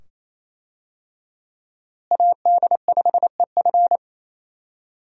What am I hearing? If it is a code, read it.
Morse code, heard as a keyed tone, about 28 wpm, 700 Hz. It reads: AD5EF
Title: USA Call Signs - 5 Characters - Repeated 2x - 28wpm